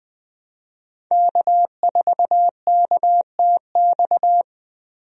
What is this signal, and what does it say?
Morse code, heard as a keyed tone, about 20 wpm, 700 Hz. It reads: K4KTX
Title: USA Call Signs - 5 Characters - 20wpm